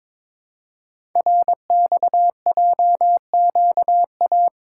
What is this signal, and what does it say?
Morse code, heard as a keyed tone, about 22 wpm, 700 Hz. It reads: RXJQA
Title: Code Groups - 5 Letters - 22wpm